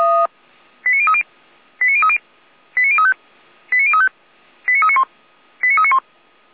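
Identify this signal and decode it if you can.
Five-tone selective call (ZVEI), heard as quick runs of five tones, tones of about 70 ms. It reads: 89030 89030 89035 89035 89391 89391
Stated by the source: ZVEI five-tone call received by SDR.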